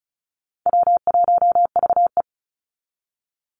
Morse code, heard as a keyed tone, about 34 wpm, 700 Hz. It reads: W1VE
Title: Call Signs - 35wpm